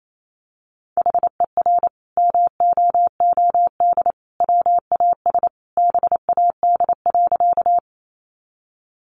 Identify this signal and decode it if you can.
Morse code, heard as a keyed tone, about 28 wpm, 700 Hz. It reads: HER MOOD WAS BAD.
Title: Sentences from Top 1100 Words 28wpm